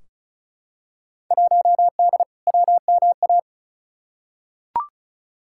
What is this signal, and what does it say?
Morse code, heard as a keyed tone, about 34 wpm, 700 Hz. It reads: 1D WMA
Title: Field Day Exchanges - 35wpm